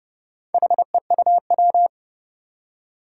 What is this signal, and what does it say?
Morse code, heard as a keyed tone, about 30 wpm, 700 Hz. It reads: HEUW
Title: Four Challenging Letter-Number Combinations 30wpm